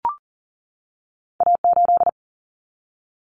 Morse code, heard as a keyed tone, about 38 wpm, 700 Hz. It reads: A8
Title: Common Cut Numbers - 2-Digit 40wpm